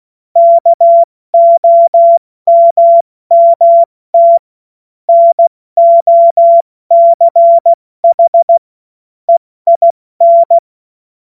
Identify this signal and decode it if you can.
Morse code, heard as a keyed tone, about 16 wpm, 680 Hz. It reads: KOMMT NOCH EIN
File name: Morsekurs / Witze / Witz 02 Januar 2024 70 F80